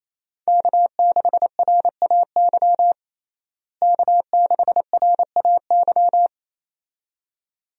Morse code, heard as a keyed tone, about 28 wpm, 700 Hz. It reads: K6RAY K6RAY
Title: USA Call Signs - 5 Characters - Repeated 2x - 28wpm